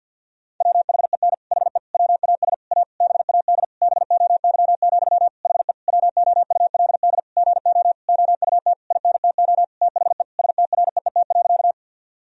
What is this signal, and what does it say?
Morse code, heard as a keyed tone, about 49 wpm, 700 Hz. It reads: WHEN HE WAS A BAD BOY, HE WOULD GO OUT INTO THE STREET.
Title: Sentences from Top 1000 Words - Review Part 3 - 50wpm